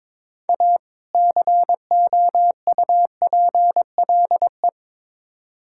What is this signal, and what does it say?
Morse code, heard as a keyed tone, about 22 wpm, 700 Hz. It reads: A COUPLE
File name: Build A Phrase - 2-Word Phrases - 22wpm